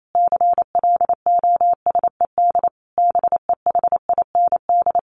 Morse code, heard as a keyed tone, about 28 wpm, 700 Hz. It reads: CLOSED BEHIND
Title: Sentences from Top 1400 Words 28wpm